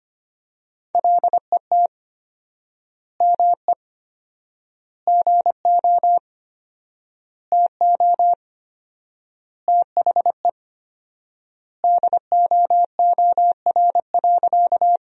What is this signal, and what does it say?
Morse code, heard as a keyed tone, about 25 wpm, 700 Hz. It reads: LET ME GO TO THE DOOR.
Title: 4x Word-Spacing - Sentences from Top 300 Words - 25wpm